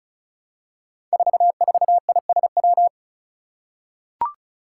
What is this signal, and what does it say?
Morse code, heard as a keyed tone, about 36 wpm, 700 Hz. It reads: 44ISW
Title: Five Challenging Letter-Number Combinations 35wpm